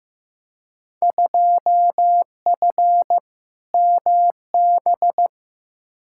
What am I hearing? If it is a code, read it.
Morse code, heard as a keyed tone, about 15 wpm, 700 Hz. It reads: 2F MB
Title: Field Day Exchanges Encore - 15wpm